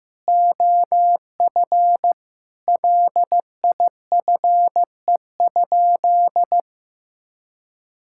Morse code, heard as a keyed tone, about 16 wpm, 700 Hz. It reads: OF LIFE?
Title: Sentences from Top 1000 Words - Review Part 5 - 15wpm